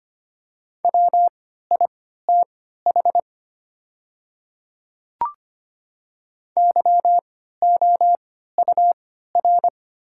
Morse code, heard as a keyed tone, about 25 wpm, spaced as for 14 wpm, 690 Hz. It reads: WITH YOUR
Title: 3x Character Spacing - Top 100 Words - 25wpm